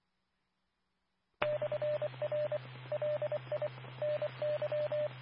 Morse code, heard as a keyed tone, about 24 wpm, 600 Hz. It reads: /R LI NY
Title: Nassau and Suffolk County Amateur Repeaters